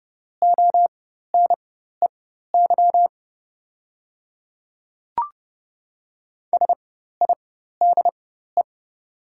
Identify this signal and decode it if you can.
Morse code, heard as a keyed tone, about 30 wpm, spaced as for 14 wpm, 690 Hz. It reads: ONEY SIDE
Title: 4x Character Spacing - Top 500 Words - 30wpm